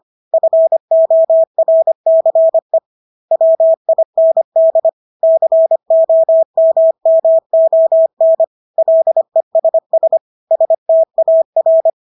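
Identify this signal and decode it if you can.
Morse code, heard as a keyed tone, about 25 wpm, 650 Hz. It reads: FORCE WIND COMMON LESS STAR
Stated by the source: Morse practice of English words at 25 wpm from CW med Gnister.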